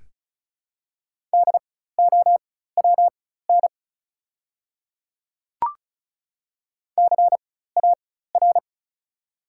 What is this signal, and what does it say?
Morse code, heard as a keyed tone, about 36 wpm, spaced as for 16 wpm, 700 Hz. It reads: DOWN CAR
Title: 4x Character Spacing - Top 300 Words - 35wpm